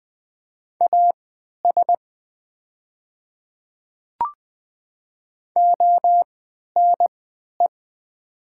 Morse code, heard as a keyed tone, about 20 wpm, spaced as for 11 wpm, 700 Hz. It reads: AS ONE
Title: 3x Character Spacing - Top 200 Words - 20wpm